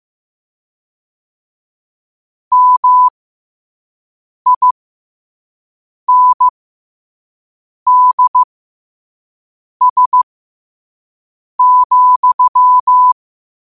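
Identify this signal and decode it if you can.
Morse code, heard as a keyed tone, about 15 wpm, spaced as for 5 wpm, 1000 Hz. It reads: MINDS,
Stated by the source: Morse code quote of the day at 5 WPM.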